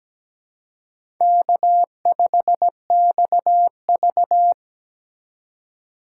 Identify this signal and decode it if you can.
Morse code, heard as a keyed tone, about 17 wpm, 700 Hz. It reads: K5XV
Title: USA Call Signs - 4 Characters - 4x Character Spacing - 17wpm